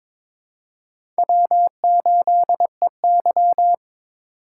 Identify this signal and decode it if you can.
Morse code, heard as a keyed tone, about 22 wpm, 700 Hz. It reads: W8EY
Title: USA Call Signs - 4 Characters - 4x Character Spacing - 22wpm